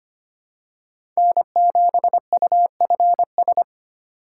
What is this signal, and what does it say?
Morse code, heard as a keyed tone, about 25 wpm, 700 Hz. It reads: N7UFS
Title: USA Call Signs - 5 Characters - 3x Character Spacing - 25wpm